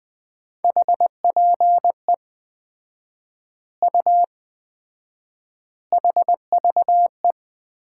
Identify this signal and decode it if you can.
Morse code, heard as a keyed tone, about 20 wpm, 700 Hz. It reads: HPE U HVE
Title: Common QSO Phrases - 4x Word-Spacing - 20wpm